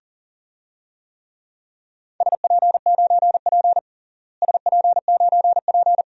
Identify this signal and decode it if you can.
Morse code, heard as a keyed tone, about 40 wpm, 700 Hz. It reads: SP9P SP9P
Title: DXCC-100 Call Signs - 4 Characters - Repeated 2x - 40wpm